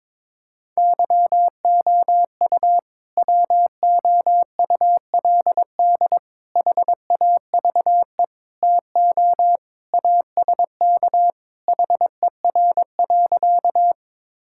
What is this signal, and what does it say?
Morse code, heard as a keyed tone, about 22 wpm, 700 Hz. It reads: YOU WOULD HAVE TO ASK HER.
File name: practice / 22wpm / Sentences with 3-Word Phrases - 22wpm